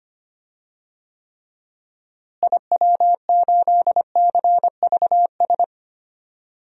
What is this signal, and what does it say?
Morse code, heard as a keyed tone, about 25 wpm, 700 Hz. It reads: IW8CVS
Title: DXCC-100 Call Signs - 6 Characters - Repeated 2x - 25wpm